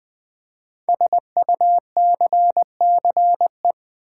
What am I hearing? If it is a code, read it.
Morse code, heard as a keyed tone, about 20 wpm, 700 Hz. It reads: SUCCE
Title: Build A Word - 5 Letters At A Time - 10-Letter Words - 20wpm